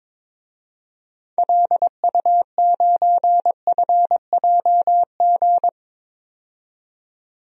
Morse code, heard as a keyed tone, about 22 wpm, 700 Hz. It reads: LU9FJG
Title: DXCC-100 Call Signs - 6 Characters - 22wpm